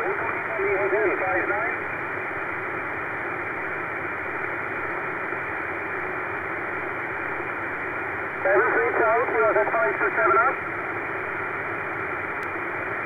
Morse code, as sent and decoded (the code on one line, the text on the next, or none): none